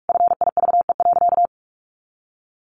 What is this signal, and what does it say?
Morse code, heard as a keyed tone, about 44 wpm, 700 Hz. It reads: FIVE.
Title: Sentences from Top 400 Words 45wpm